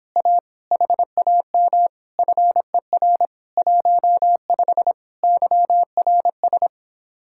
Morse code, heard as a keyed tone, about 26 wpm, 700 Hz. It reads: A HAM FER 15 YRS